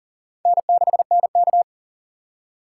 Morse code, heard as a keyed tone, about 37 wpm, 700 Hz. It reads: N6NK
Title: USA Call Signs - 4 Characters - 4x Character Spacing - 40wpm